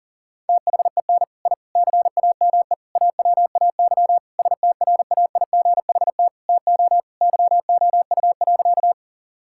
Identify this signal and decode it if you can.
Morse code, heard as a keyed tone, about 40 wpm, 700 Hz. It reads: THEN I CAME AWAY STRAIGHT TO YOU.
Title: Sentences from Top 1000 Words - Review Part 6 - 40wpm